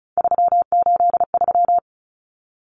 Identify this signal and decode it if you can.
Morse code, heard as a keyed tone, about 34 wpm, 700 Hz. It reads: 383
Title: Numbers - 3-Digit 35wpm